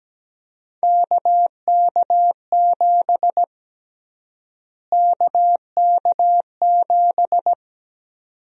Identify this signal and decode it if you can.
Morse code, heard as a keyed tone, about 17 wpm, 700 Hz. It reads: KK7 KK7
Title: USA Call Sign Prefixes - Repeated 2x - 17wpm